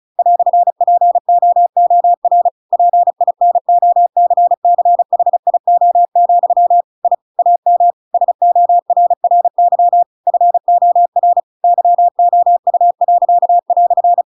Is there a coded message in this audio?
Yes: Morse code from 0.2 to 14.2 s, heard as a keyed tone, about 34 wpm, 700 Hz. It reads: "POOR PINOCCHIO, I AM SORRY FOR YOU."